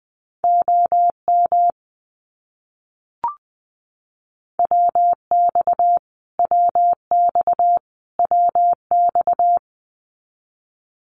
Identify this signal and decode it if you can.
Morse code, heard as a keyed tone, about 20 wpm, 700 Hz. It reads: OM WX WX WX
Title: Abbreviations Repeated 3x - 20wpm